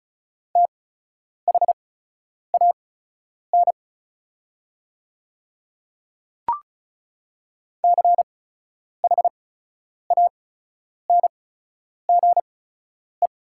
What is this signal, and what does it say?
Morse code, heard as a keyed tone, about 36 wpm, spaced as for 10 wpm, 700 Hz. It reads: THAN CHANGE
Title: 8x Character Spacing - Top 300 Words - 35wpm